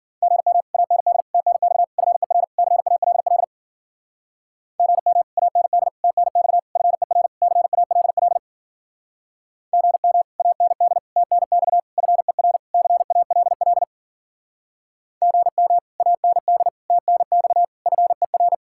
Morse code, heard as a keyed tone, about 48 wpm, 700 Hz. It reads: GM AND TNX FER CALL GM AND TNX FER CALL GM AND TNX FER CALL GM AND TNX FER
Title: Speed-Racing Common QSO Phrases - 40wpm